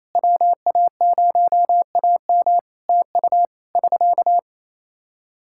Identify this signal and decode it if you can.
Morse code, heard as a keyed tone, about 28 wpm, 700 Hz. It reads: WA0AM TU <SK>
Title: Practice QSO - Exchange 4 - Ending - 28wpm